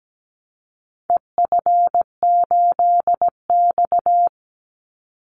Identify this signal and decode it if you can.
Morse code, heard as a keyed tone, about 17 wpm, 700 Hz. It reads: EF8X
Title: Call Signs - 17wpm